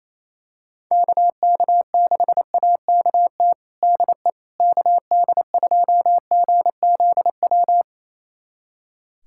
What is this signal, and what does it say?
Morse code, heard as a keyed tone, about 27 wpm, 700 Hz. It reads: KK6AKT DE KD2GZW
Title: USA Callsign DE Callsign - 6 Characters - 28wpm